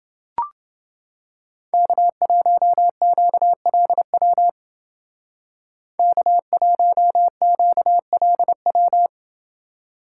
Morse code, heard as a keyed tone, about 31 wpm, 700 Hz. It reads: K1QLW K1QLW